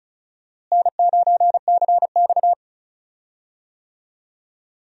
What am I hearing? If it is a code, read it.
Morse code, heard as a keyed tone, about 34 wpm, 700 Hz. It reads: N9CX
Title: USA Call Signs - 4 Characters - Repeated 2x - 35wpm